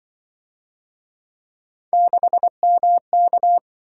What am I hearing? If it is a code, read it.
Morse code, heard as a keyed tone, about 26 wpm, 700 Hz. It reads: 6MK